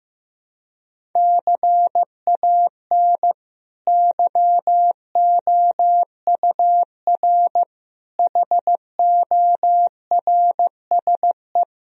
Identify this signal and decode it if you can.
Morse code, heard as a keyed tone, about 16 wpm, 700 Hz. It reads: CAN YOUR HORSE
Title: Sentences from Top 1000 Words - Review Part 1 - 15wpm